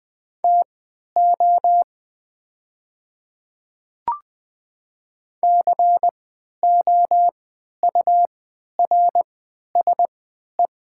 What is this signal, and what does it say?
Morse code, heard as a keyed tone, about 21 wpm, spaced as for 11 wpm, 700 Hz. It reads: TO COURSE
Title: 3x Character Spacing - Top 500 Words - 20wpm